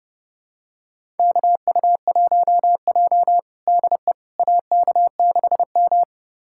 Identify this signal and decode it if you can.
Morse code, heard as a keyed tone, about 32 wpm, 700 Hz. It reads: KU1J DE AK6M
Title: USA Callsign DE Callsign - 4 Characters - 30wpm